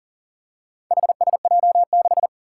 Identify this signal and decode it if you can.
Morse code, heard as a keyed tone, about 40 wpm, 700 Hz. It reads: HSJ6
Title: Four Challenging Letter-Number Combinations 40wpm